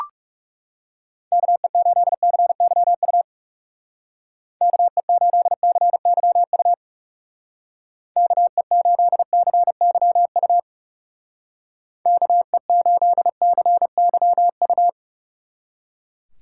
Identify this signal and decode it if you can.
Morse code, heard as a keyed tone, about 38 wpm, 700 Hz. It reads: KE8CYU KE8CYU KE8CYU KE8CYU